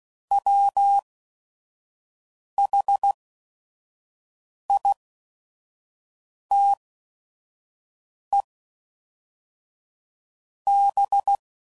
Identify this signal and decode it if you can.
Morse code, heard as a keyed tone, about 16 wpm, spaced as for 5 wpm, 800 Hz. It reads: WHITEB
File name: CW Code Groups @5WPM / fromSM's / Cities